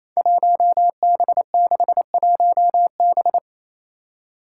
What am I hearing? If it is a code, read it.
Morse code, heard as a keyed tone, about 28 wpm, 700 Hz. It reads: 1B61B
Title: Five Challenging Letter-Number Combinations 28wpm